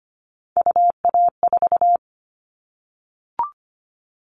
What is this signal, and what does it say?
Morse code, heard as a keyed tone, about 25 wpm, 700 Hz. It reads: UA4
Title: Callsign Prefixes for Top 25 DXCC - Repeated 3x - 25wpm